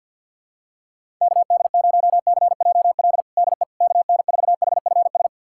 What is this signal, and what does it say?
Morse code, heard as a keyed tone, about 51 wpm, 700 Hz. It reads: KD0CJL DE KN4HFS
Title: USA Callsign DE Callsign 50wpm